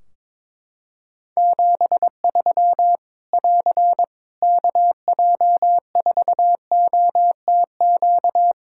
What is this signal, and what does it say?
Morse code, heard as a keyed tone, about 22 wpm, 700 Hz. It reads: 73 <AR> KJ4OTQ
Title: Practice QSO - Exchange 4 - Ending - 22wpm